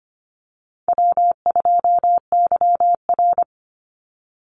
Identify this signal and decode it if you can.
Morse code, heard as a keyed tone, about 25 wpm, 700 Hz. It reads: W2YR
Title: Easy Call Signs - 25wpm